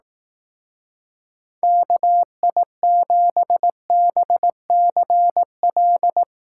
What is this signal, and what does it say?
Morse code, heard as a keyed tone, about 18 wpm, 700 Hz. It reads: KI7BCL
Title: Speed Racing - USA 2x3 Call Signs 15wpm